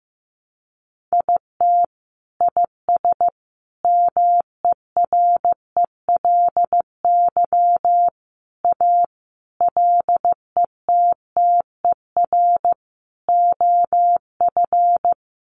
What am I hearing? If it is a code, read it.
Morse code, heard as a keyed tone, about 15 wpm, 700 Hz. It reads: IT IS MERELY A LETTER OF
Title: Sentences from Top 1400 Words 15wpm